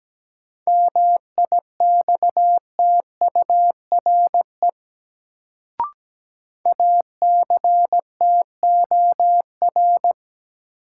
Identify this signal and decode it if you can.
Morse code, heard as a keyed tone, about 17 wpm, 700 Hz. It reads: MIXTURE ACTOR
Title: Sentences from Top 1500 Words - New Word Drill Alt - 17wpm